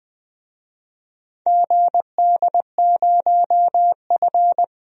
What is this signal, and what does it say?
Morse code, heard as a keyed tone, about 20 wpm, 700 Hz. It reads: GD0F
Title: DXCC-100 Call Signs - 4 Characters - 20wpm